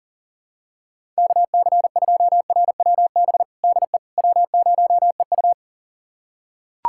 Morse code, heard as a keyed tone, about 37 wpm, 700 Hz. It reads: KC2RWB DE W0EU
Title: USA Callsign DE Callsign 40wpm